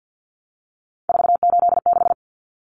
Morse code, heard as a keyed tone, about 50 wpm, 700 Hz. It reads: 486
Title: Common Cut Numbers - 3-Digit 50wpm